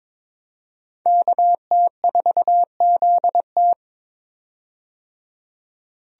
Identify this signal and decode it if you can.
Morse code, heard as a keyed tone, about 22 wpm, 700 Hz. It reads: KT4ZT
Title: USA Call Signs - 5 Characters - 22wpm